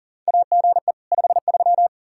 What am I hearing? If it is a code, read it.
Morse code, heard as a keyed tone, about 40 wpm, 700 Hz. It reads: AGE 53
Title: K1USN SST QSO Practice 40wpm